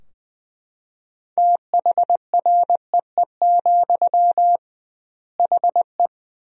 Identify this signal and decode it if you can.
Morse code, heard as a keyed tone, about 20 wpm, 700 Hz. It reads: THREE, HE
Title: 2x Word-Spacing - Sentences from Top 200 Words - 20wpm